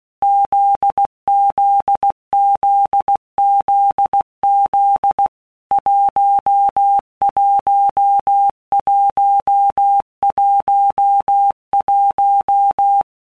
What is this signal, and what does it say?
Morse code, heard as a keyed tone, about 16 wpm, 800 Hz. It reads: ZZZZZ 11111
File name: Basic Letter Groups By Blocks / Groups@15 / JZ17@15